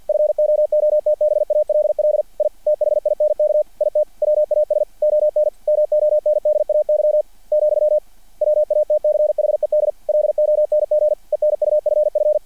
Morse code, heard as a keyed tone, about 50 wpm, 600 Hz. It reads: FOOTBALL I THINK IT WAS ON MONDAY , WATCHED LONG ENUFF
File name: archived / CRSnet